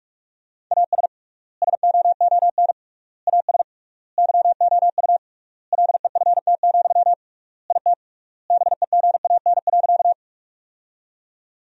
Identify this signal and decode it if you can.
Morse code, heard as a keyed tone, about 46 wpm, 700 Hz. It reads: AS SOON AS YOU LEFT, IT BEGAN.
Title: 3x Word-Spacing - Sentences from Top 400 Words - 45wpm